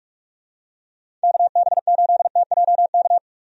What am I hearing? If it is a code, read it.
Morse code, heard as a keyed tone, about 45 wpm, 700 Hz. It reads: KB8TJK
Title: USA Call Signs - 6 Characters - 45wpm